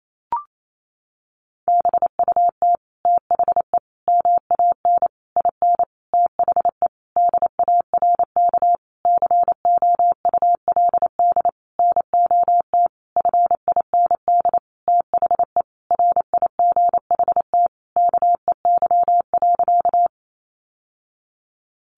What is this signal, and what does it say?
Morse code, heard as a keyed tone, about 28 wpm, 700 Hz. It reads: BUT THE MAN IN THE DARK COULD NOT FIND THE RIGHT KEY.
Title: Sentences from Top 600 Words 28wpm